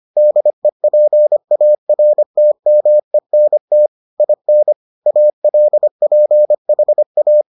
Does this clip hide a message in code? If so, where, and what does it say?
Yes: Morse code from 0.2 to 7.4 s, heard as a keyed tone, about 25 wpm, 600 Hz. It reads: DEPARTMENT IN ALPHA